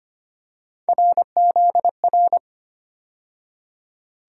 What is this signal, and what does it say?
Morse code, heard as a keyed tone, about 25 wpm, 700 Hz. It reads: RZR
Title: USA Call Sign Postfixes - 3 Letters 25wpm